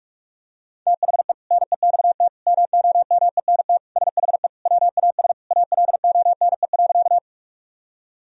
Morse code, heard as a keyed tone, about 44 wpm, 700 Hz. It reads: THE NEXT MOMENT SHE WAS ALONE.
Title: Sentences from Top 1000 Words - Review Part 1 - 45wpm